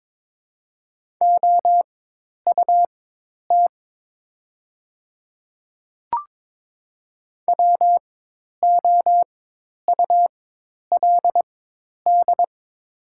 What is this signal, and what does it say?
Morse code, heard as a keyed tone, about 22 wpm, spaced as for 10 wpm, 700 Hz. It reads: OUT WOULD
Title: 4x Character Spacing - Top 100 Words - 22wpm